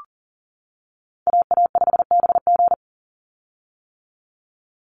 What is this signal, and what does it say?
Morse code, heard as a keyed tone, about 40 wpm, 700 Hz. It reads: AA5BG
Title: Call Signs - 40wpm